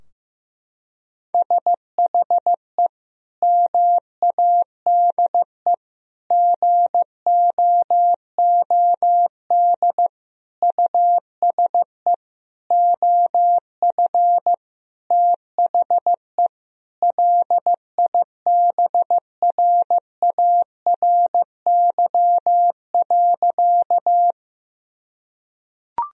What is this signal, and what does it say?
Morse code, heard as a keyed tone, about 15 wpm, 700 Hz. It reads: SHE MADE GOOD USE OF THE LIBRARY.
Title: Sentences from Top 1000 Words - Review Part 4 - 15wpm